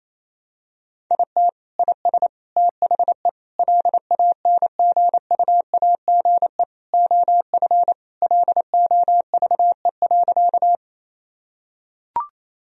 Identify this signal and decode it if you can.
Morse code, heard as a keyed tone, about 28 wpm, 700 Hz. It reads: IT IS THE LANGUAGE OF LOVE.
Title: Sentences from Top 1000 Words - Review Part 3 - 28wpm